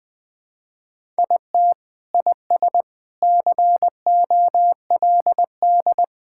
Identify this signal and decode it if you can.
Morse code, heard as a keyed tone, about 20 wpm, 700 Hz. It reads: IT IS COLD
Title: Sentences with 3-Word Phrases - 20wpm